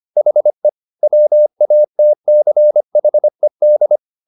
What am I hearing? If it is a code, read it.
Morse code, heard as a keyed tone, about 25 wpm, 600 Hz. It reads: HE WATCHED
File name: the old man and the sea part 1 25wpm